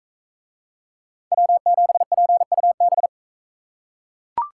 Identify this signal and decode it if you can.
Morse code, heard as a keyed tone, about 43 wpm, 700 Hz. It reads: W7PUB